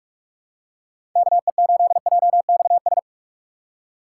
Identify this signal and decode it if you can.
Morse code, heard as a keyed tone, about 46 wpm, 690 Hz. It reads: KE8JXS
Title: USA Call Signs - 6 Characters - 3x Character Spacing - 45wpm